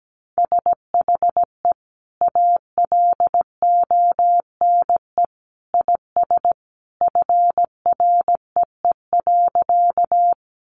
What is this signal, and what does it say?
Morse code, heard as a keyed tone, about 17 wpm, 700 Hz. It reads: SHE ALONE IS FREE.
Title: Sentences from Top 1000 Words 17wpm